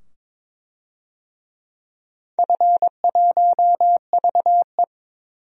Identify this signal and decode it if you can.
Morse code, heard as a keyed tone, about 21 wpm, 700 Hz. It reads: F1VE
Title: DXCC-100 Call Signs - 4 Characters - 22wpm